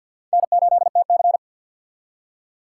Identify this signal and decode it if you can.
Morse code, heard as a keyed tone, about 62 wpm, 700 Hz. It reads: N8TC
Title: Speed Racing - USA 1x2 Call Signs 50wpm